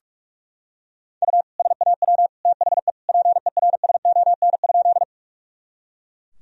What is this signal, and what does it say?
Morse code, heard as a keyed tone, about 41 wpm, 700 Hz. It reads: U SAW THE PERSON?
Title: Sentences from Top 400 Words and Abbreviations 45wpm